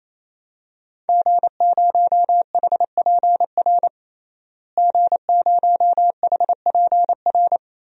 Morse code, heard as a keyed tone, about 27 wpm, 700 Hz. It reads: G0HPR G0HPR
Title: DXCC-100 Call Signs - 5 Characters - Repeated 3x - 28wpm